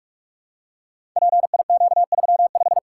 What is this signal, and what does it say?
Morse code, heard as a keyed tone, about 44 wpm, 700 Hz. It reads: PIQ35
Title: Code Groups - 5 Letter:Numbers - 45wpm